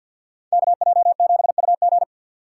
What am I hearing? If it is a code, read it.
Morse code, heard as a keyed tone, about 47 wpm, 700 Hz. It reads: KJ7UG
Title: USA Call Signs - 5 Characters - 4x Character Spacing - 50wpm